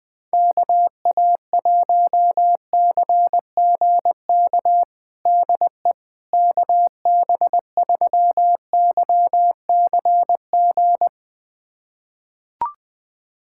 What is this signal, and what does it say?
Morse code, heard as a keyed tone, about 20 wpm, 700 Hz. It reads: KA1CGK DE KB3YCG
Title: USA Callsign DE Callsign - 6 Characters - 20wpm